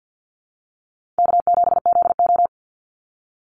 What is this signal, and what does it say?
Morse code, heard as a keyed tone, about 50 wpm, 700 Hz. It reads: K7ZO
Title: NA SOTA Chaser Callsigns 50wpm